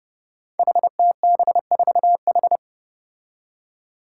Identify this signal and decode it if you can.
Morse code, heard as a keyed tone, about 30 wpm, 700 Hz. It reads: HTB4H
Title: Five Challenging Letter-Number Combinations 30wpm